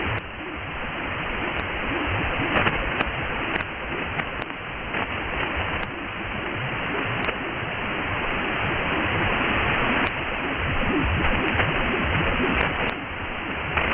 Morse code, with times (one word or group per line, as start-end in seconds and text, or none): none